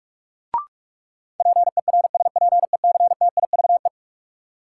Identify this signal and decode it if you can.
Morse code, heard as a keyed tone, about 46 wpm, 700 Hz. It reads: PERSPECTIVE
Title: Sentences from Top 1200 Words - New Word Drill - 45wpm